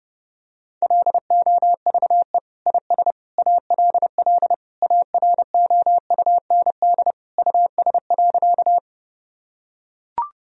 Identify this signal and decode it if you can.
Morse code, heard as a keyed tone, about 30 wpm, 700 Hz. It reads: LOVE IS ALL AROUND US.
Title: Sentences from Top 1000 Words - Review Part 3 - 30wpm